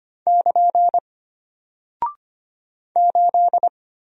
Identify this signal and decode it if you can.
Morse code, heard as a keyed tone, about 25 wpm, 700 Hz. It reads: <KN> 8
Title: Single Character (Letter:Number:Punctuation:Prosign) Alt 25wpm